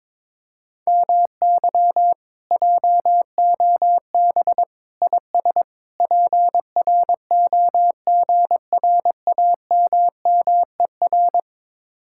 Morse code, heard as a keyed tone, about 22 wpm, 700 Hz. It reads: MY JOB IS PROGRAMMER
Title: Speed-Racing Common QSO Phrases - 22wpm